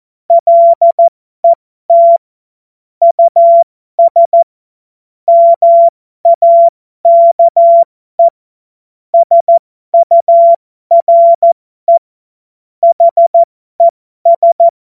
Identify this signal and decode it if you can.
Morse code, heard as a keyed tone, about 14 wpm, spaced as for 12 wpm, 680 Hz. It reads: LET US MAKE SURE HES